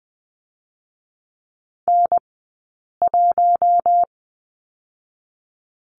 Morse code, heard as a keyed tone, about 19 wpm, 700 Hz. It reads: N 1
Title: 040 Lesson 8d - Number 1 - Two Characters 20wpm